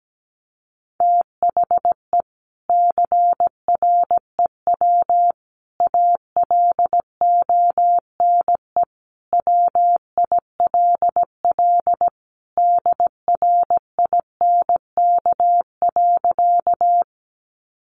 Morse code, heard as a keyed tone, about 17 wpm, 700 Hz. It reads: THE CREW ALONE WILL DRINK.
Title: Sentences from Top 1500 Words 17wpm